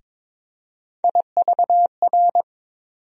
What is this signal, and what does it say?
Morse code, heard as a keyed tone, about 22 wpm, 700 Hz. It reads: IVR
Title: USA Call Sign Postfixes - 3 Letters 22wpm